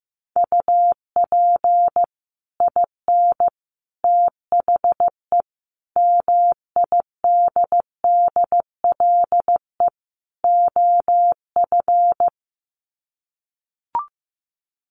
Morse code, heard as a keyed tone, about 15 wpm, 700 Hz. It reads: UP IN THE MIDDLE OF
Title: Sets of 5 Words Encore 15wpm